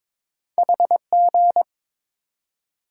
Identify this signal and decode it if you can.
Morse code, heard as a keyed tone, about 22 wpm, 700 Hz. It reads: HG
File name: practice / 22wpm / Two Letter-Number Combinations - 22wpm